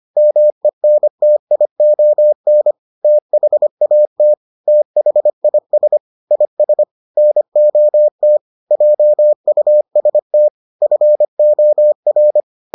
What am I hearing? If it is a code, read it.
Morse code, heard as a keyed tone, about 25 wpm, 600 Hz. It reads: MENTION THAT THIS IS NOT JUST FOR